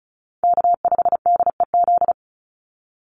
Morse code, heard as a keyed tone, about 34 wpm, 700 Hz. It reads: K5DEZ
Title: NA SOTA Chaser Callsigns 35wpm